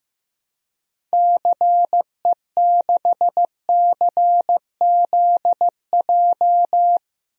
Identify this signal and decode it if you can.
Morse code, heard as a keyed tone, about 15 wpm, 700 Hz. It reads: CE6CZJ
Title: DXCC-100 Call Signs - 6 Characters - 15wpm